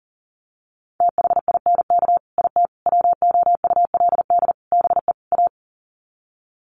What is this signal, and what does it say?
Morse code, heard as a keyed tone, about 40 wpm, 700 Hz. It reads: THINK IT WOULD BE A
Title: Sets of 5 Words Encore2 40wpm